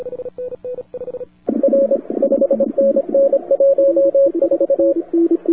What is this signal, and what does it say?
Morse code, heard as a keyed tone, about 25 wpm, 550 Hz, with radio noise. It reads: R 5NN 14
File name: wwcw08 / k5zd wwcw08 1400z